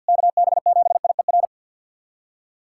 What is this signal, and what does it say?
Morse code, heard as a keyed tone, about 51 wpm, 700 Hz. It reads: KB7IER
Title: USA Callsign DE Callsign - 6 Characters - 50wpm